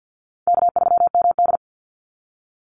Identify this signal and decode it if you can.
Morse code, heard as a keyed tone, about 50 wpm, 700 Hz. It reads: K3MD
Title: Easy Call Signs Repeated 3x - 50wpm